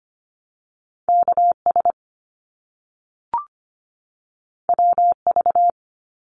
Morse code, heard as a keyed tone, about 25 wpm, 700 Hz. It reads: KS WV
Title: US States 25wpm